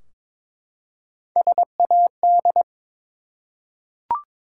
Morse code, heard as a keyed tone, about 22 wpm, 700 Hz. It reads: SAD
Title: Three Letter Words - 22wpm